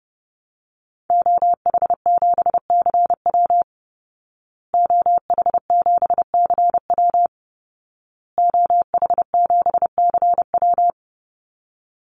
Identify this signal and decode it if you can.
Morse code, heard as a keyed tone, about 30 wpm, 700 Hz. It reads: OH7CW OH7CW OH7CW
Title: Call Signs Repeated 2x - 30wpm